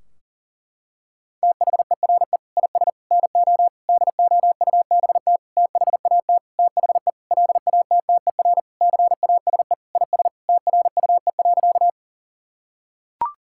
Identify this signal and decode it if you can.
Morse code, heard as a keyed tone, about 40 wpm, 700 Hz. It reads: THERE IS NO DOUBT THAT THE LATTER CASE IS TRUE.
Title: Sentences from Top 1600 Words 40wpm